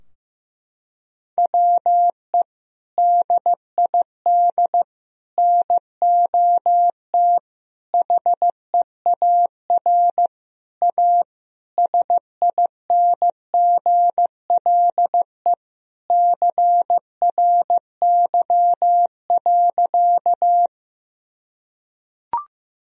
Morse code, heard as a keyed tone, about 15 wpm, 700 Hz. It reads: WE DID NOT HEAR A SINGLE CRY.
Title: Sentences from Top 1000 Words - Review Part 4 - 15wpm